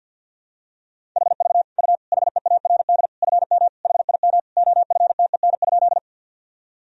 Morse code, heard as a keyed tone, about 49 wpm, 700 Hz. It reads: HV U HEARD FM HIM OFTEN?
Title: Sentences from Top 400 Words and Abbreviations 50wpm